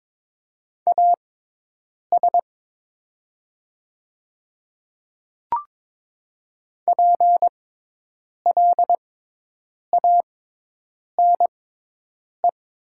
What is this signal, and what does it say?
Morse code, heard as a keyed tone, about 22 wpm, spaced as for 8 wpm, 700 Hz. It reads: AS PLANE
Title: 6x Character Spacing - Top 500 Words - 22wpm